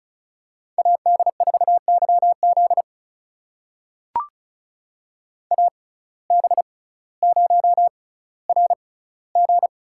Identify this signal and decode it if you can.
Morse code, heard as a keyed tone, about 36 wpm, spaced as for 18 wpm, 700 Hz. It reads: AD4YZ AB0RG
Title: USA Call Signs - 5 Characters - 6x Character Spacing - 35wpm